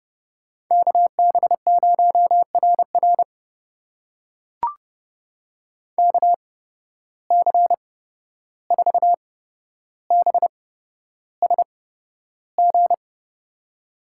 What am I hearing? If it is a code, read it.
Morse code, heard as a keyed tone, about 32 wpm, spaced as for 12 wpm, 700 Hz. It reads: KB0RR KC4BSG
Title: USA Call Signs - 8x Character Spacing - 30wpm